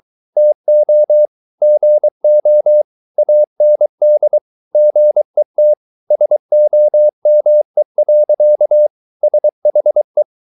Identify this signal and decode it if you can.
Morse code, heard as a keyed tone, about 23 wpm, 600 Hz. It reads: TO GO AND GET SOME. SHE